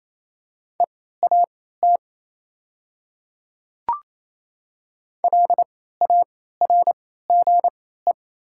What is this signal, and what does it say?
Morse code, heard as a keyed tone, about 27 wpm, spaced as for 16 wpm, 700 Hz. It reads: EAT LARGE
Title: 3x Character Spacing - Top 300 Words - 28wpm